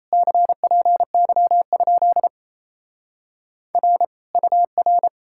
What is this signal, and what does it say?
Morse code, heard as a keyed tone, about 37 wpm, 700 Hz. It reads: CPY? R UR